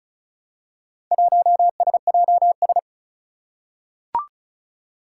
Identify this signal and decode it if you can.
Morse code, heard as a keyed tone, about 36 wpm, 700 Hz. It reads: 1SJS
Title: Four Challenging Letter-Number Combinations 35wpm